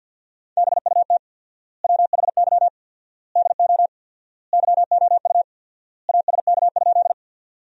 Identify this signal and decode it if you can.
Morse code, heard as a keyed tone, about 50 wpm, 700 Hz. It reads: BUT WHY DO YOU ASK?
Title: 4x Word-Spacing - Sentences from Top 200 Words - 50wpm